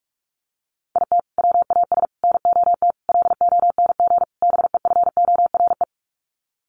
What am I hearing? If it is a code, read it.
Morse code, heard as a keyed tone, about 44 wpm, 700 Hz. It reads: IT WAS NOT LONG BEFORE
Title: Sets of 5 Words Encore2 45wpm